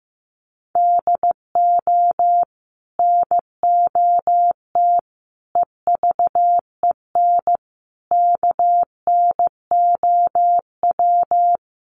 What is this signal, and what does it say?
Morse code, heard as a keyed tone, about 15 wpm, 700 Hz. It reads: DO NOT EVEN KNOW
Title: Sets of 4 Words 15wpm